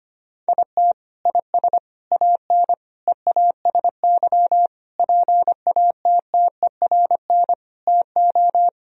Morse code, heard as a keyed tone, about 25 wpm, 700 Hz. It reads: IT IS AN EASY PATTERN TO
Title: Sentences from Top 1000 Words - Review Part 2 - 25wpm